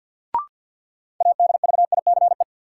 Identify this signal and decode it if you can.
Morse code, heard as a keyed tone, about 47 wpm, 700 Hz. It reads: ADVICE
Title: Sentences from Top 1100 Words - New Word Drill Alt - 50wpm